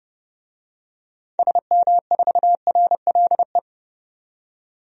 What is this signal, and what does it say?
Morse code, heard as a keyed tone, about 30 wpm, 700 Hz. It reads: SM4RLE
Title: DXCC-100 Call Signs - 6 Characters - 30wpm